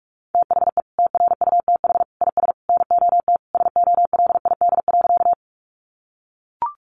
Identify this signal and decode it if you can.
Morse code, heard as a keyed tone, about 44 wpm, 700 Hz. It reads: THE TRUTH IS NOT SOLID.
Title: Sentences from Top 1500 Words 45wpm